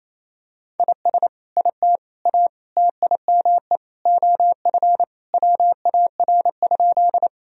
Morse code, heard as a keyed tone, about 28 wpm, 700 Hz. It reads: IS IT A TIME OF WAR?
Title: Sentences from Top 1000 Words - Review Part 2 - 28wpm